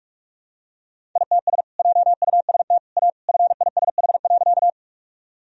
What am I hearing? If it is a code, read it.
Morse code, heard as a keyed tone, about 46 wpm, 700 Hz. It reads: ITS JUST A FISH.
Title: Sentences from Top 1000 Words - Review Part 1 - 45wpm